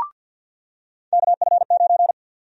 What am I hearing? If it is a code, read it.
Morse code, heard as a keyed tone, about 48 wpm, 700 Hz. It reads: KF9
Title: USA Call Sign Prefixes - 50wpm